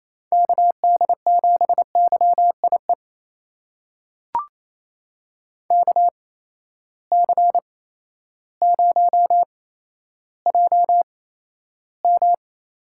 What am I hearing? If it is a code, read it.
Morse code, heard as a keyed tone, about 28 wpm, spaced as for 13 wpm, 700 Hz. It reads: KD7YIE KC0JM
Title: USA Call Signs - 6 Characters - 8x Character Spacing - 28wpm